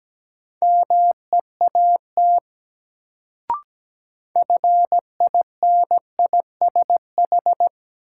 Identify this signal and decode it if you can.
Morse code, heard as a keyed tone, about 17 wpm, 700 Hz. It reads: MEAT FINISH
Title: Sentences from Top 1000 Words - Review Part 3 - New Word Drill - 17wpm